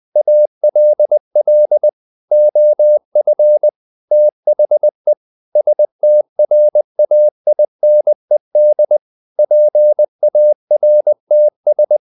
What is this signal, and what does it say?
Morse code, heard as a keyed tone, about 20 wpm, 600 Hz. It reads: ALL OF THE STRAINED PARTS
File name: the old man and the sea part 2 20wpm